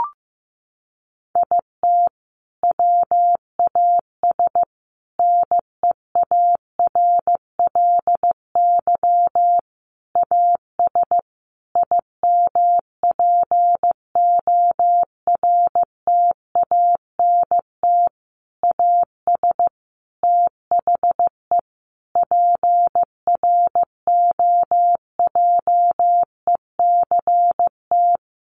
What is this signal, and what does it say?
Morse code, heard as a keyed tone, about 15 wpm, 700 Hz. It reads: IT WAS NEARLY AS IMPORTANT AS THE PROJECT
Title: Sentences from Top 1400 Words 15wpm